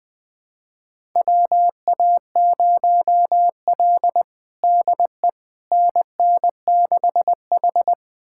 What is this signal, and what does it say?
Morse code, heard as a keyed tone, about 20 wpm, 700 Hz. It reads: WA0L DE NN6H
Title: USA Callsign DE Callsign - 4 Characters - 20wpm